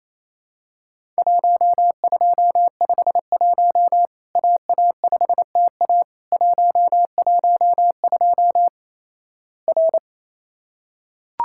Morse code, heard as a keyed tone, about 28 wpm, 700 Hz. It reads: 1251 AA5TA 112 R
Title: Worked All Europe DX Contest - QTC Exchange - 28wpm